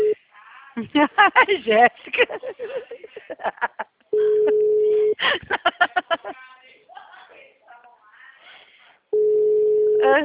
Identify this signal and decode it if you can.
Ringback tone (Europe, 425 Hz), with speech.
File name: out- -2046- - - .50618